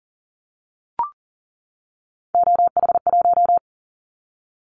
Morse code, heard as a keyed tone, about 40 wpm, 700 Hz. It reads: OH1
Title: Callsign Prefixes for Top 50 DXCC - 40wpm